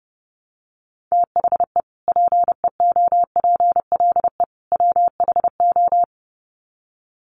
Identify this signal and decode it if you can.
Morse code, heard as a keyed tone, about 30 wpm, 700 Hz. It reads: THE PEOPLE WHO
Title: Sets of 3 Words 30wpm